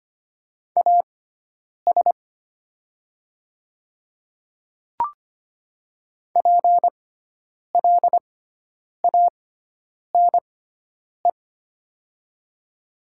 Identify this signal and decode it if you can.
Morse code, heard as a keyed tone, about 25 wpm, spaced as for 9 wpm, 700 Hz. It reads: AS PLANE
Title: 6x Character Spacing - Top 500 Words - 25wpm